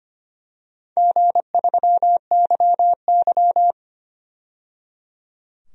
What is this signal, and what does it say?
Morse code, heard as a keyed tone, about 25 wpm, 700 Hz. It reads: G3YY
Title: DXCC-100 Call Signs - 4 Characters - 25wpm